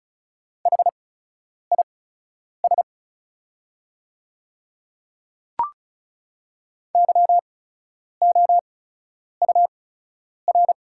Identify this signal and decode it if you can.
Morse code, heard as a keyed tone, about 36 wpm, spaced as for 10 wpm, 700 Hz. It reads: HIS YOUR
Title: 8x Character Spacing - Top 100 Words - 35wpm